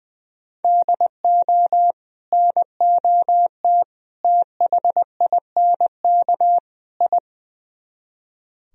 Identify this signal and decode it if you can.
Morse code, heard as a keyed tone, about 20 wpm, 700 Hz. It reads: DO NOT THINK I
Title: Speed-Racing - Sets of 4 Words 20wpm